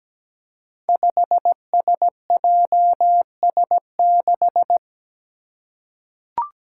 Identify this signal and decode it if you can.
Morse code, heard as a keyed tone, about 17 wpm, 700 Hz. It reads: 5SJS6
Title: Five Challenging Letter-Number Combinations 17wpm